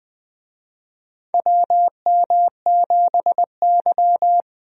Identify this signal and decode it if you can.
Morse code, heard as a keyed tone, about 20 wpm, 700 Hz. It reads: WM7Y
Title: USA Call Signs - 4 Characters - 20wpm